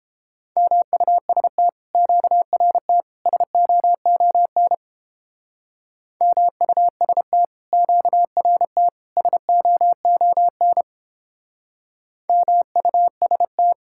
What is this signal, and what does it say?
Morse code, heard as a keyed tone, about 31 wpm, 700 Hz. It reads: MUST QRT SOON MUST QRT SOON MUST